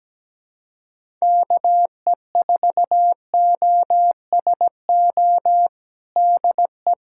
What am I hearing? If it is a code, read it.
Morse code, heard as a keyed tone, about 17 wpm, 700 Hz. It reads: KE4OSO DE
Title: USA Callsign DE Callsign - 6 Characters - 17wpm